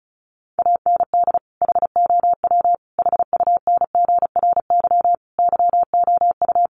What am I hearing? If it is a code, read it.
Morse code, heard as a keyed tone, about 34 wpm, 700 Hz. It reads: AND HOW HUNGRY YOU
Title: Sentences from Top 1400 Words 35wpm